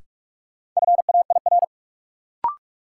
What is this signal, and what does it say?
Morse code, heard as a keyed tone, about 46 wpm, 700 Hz. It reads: FAIR
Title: Sentences from Top 1000 Words - Review Part 5 - New Word Drill - 45wpm